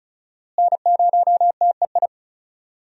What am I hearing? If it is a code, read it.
Morse code, heard as a keyed tone, about 36 wpm, 700 Hz. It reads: N0TEI
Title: USA Call Signs - 5 Characters - Repeated 3x - 35wpm